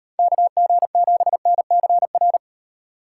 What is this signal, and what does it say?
Morse code, heard as a keyed tone, about 38 wpm, 700 Hz. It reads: KG7NCR